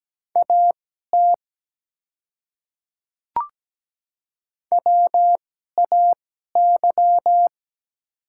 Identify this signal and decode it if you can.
Morse code, heard as a keyed tone, about 17 wpm, spaced as for 12 wpm, 690 Hz. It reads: AT WAY
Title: 2x Character Spacing - Top 100 Words - 17wpm